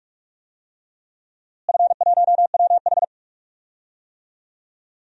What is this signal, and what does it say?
Morse code, heard as a keyed tone, about 41 wpm, 700 Hz. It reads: F1WH
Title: DXCC-100 Call Signs - 4 Characters - Repeated 3x - 45wpm